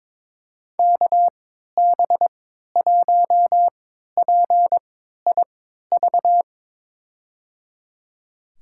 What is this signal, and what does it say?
Morse code, heard as a keyed tone, about 22 wpm, spaced as for 13 wpm, 700 Hz. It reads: KB1PIV
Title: USA Call Signs - 3x Character Spacing - 22wpm